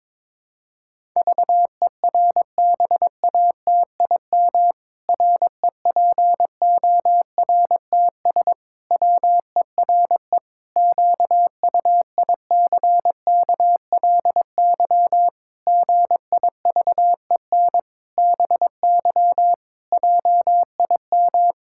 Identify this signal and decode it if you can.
Morse code, heard as a keyed tone, about 22 wpm, 700 Hz. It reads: VERBATIM REPORTS WERE QUICKLY GIVEN BY JIM
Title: Pangram Challenge v2 - 22wpm